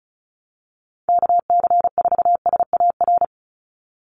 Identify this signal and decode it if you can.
Morse code, heard as a keyed tone, about 34 wpm, 700 Hz. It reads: KC4SAR
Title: USA Call Signs 35wpm